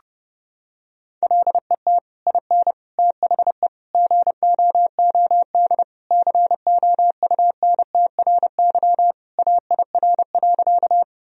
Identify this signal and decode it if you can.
Morse code, heard as a keyed tone, about 32 wpm, 700 Hz. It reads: LET IN THE GOOD COUNTRY AIR.
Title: Sentences from Top 1000 Words - Review Part 1 - 30wpm